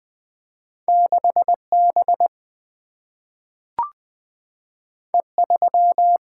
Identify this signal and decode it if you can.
Morse code, heard as a keyed tone, about 20 wpm, 700 Hz. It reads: 6B E3
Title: Single-Letter Single-Number 20wpm